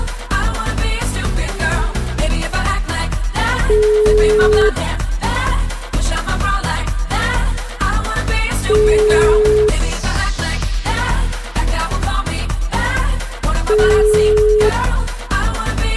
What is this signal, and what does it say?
Ringback tone (Europe, 425 Hz), with noise.